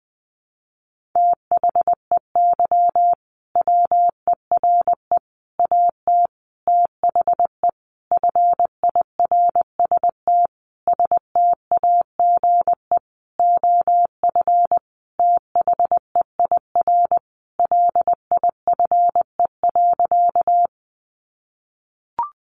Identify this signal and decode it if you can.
Morse code, heard as a keyed tone, about 19 wpm, 700 Hz. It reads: THEY WERE AT THE FIRST STAGE OF THEIR LIFE.
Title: Sentences from Top 1200 Words 20wpm